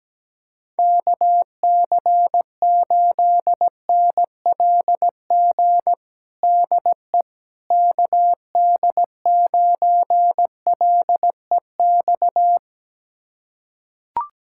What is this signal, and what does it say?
Morse code, heard as a keyed tone, about 17 wpm, 700 Hz. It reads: KC8NLG DE KD9LEX
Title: USA Callsign DE Callsign - 6 Characters - 17wpm